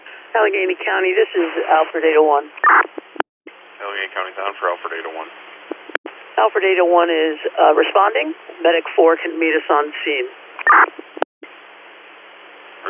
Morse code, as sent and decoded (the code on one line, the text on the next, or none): none